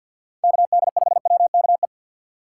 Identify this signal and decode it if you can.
Morse code, heard as a keyed tone, about 50 wpm, 700 Hz. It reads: KD5WKE
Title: USA Call Signs - 6 Characters - 6x Character Spacing - 50wpm